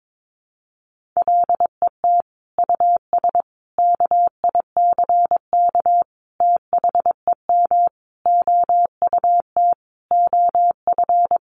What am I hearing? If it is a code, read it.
Morse code, heard as a keyed tone, about 22 wpm, 700 Hz. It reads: LET US KICK THEM OUT OF
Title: Sentences from Top 1300 Words 22wpm